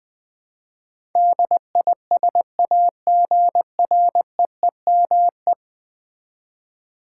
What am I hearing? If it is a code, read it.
Morse code, heard as a keyed tone, about 20 wpm, 700 Hz. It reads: DISAGREEME
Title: Build A Word - 5 Letters At A Time - 12-Letter Words - 20wpm